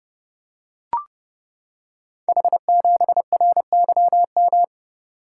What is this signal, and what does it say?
Morse code, heard as a keyed tone, about 30 wpm, 700 Hz. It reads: H7RYM
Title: Code Groups - 5 Letter:Numbers - 30wpm